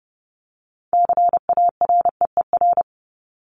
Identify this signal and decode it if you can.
Morse code, heard as a keyed tone, about 30 wpm, 700 Hz. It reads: CAREER
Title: Top 751-1000 Words 30wpm